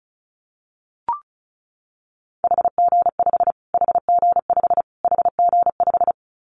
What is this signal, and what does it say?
Morse code, heard as a keyed tone, about 34 wpm, 700 Hz. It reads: HG5 HG5 HG5
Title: Callsign Prefixes for Top 25 DXCC - Repeated 3x - 35wpm